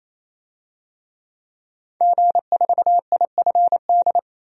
Morse code, heard as a keyed tone, about 27 wpm, 700 Hz. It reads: G4IFD
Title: DXCC-100 Call Signs - 5 Characters - 28wpm